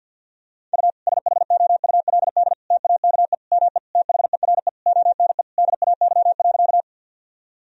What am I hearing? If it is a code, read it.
Morse code, heard as a keyed tone, about 50 wpm, 700 Hz. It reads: U SHOULD TAKE ME THERE ONE DAY.
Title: Sentences from Top 300 Words and Abbreviations 50wpm